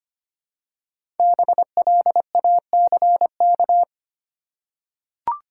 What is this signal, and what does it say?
Morse code, heard as a keyed tone, about 25 wpm, 700 Hz. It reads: BLACK
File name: 5 Letter Words - First 3 Letters Given - 25wpm